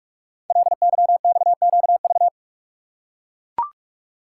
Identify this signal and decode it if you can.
Morse code, heard as a keyed tone, about 44 wpm, 690 Hz. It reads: LYXQV
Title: Code Groups - 5 Letters - 45wpm